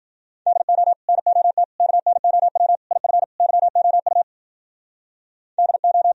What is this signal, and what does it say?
Morse code, heard as a keyed tone, about 47 wpm, 700 Hz. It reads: DO NOT KNOW IF YOU DO
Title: Speed-Racing - Sets of 5 Words 40wpm